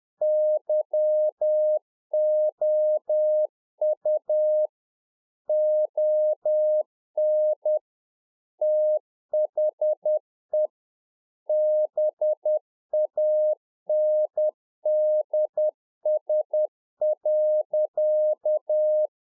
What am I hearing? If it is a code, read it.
Morse code, heard as a keyed tone, about 10 wpm, 610 Hz. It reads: YOU ON THE BANDS.
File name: 2011-04-mp3 / RICK JJ0MPI 10WPM